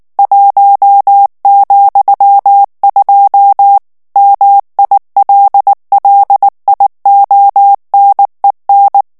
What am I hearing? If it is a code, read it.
Morse code, heard as a keyed tone, about 19 wpm, 800 Hz. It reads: 1,2 MILLIONEN
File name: morse mp3 / ALLEMAND texte clair 2 vitesse 18 mots minute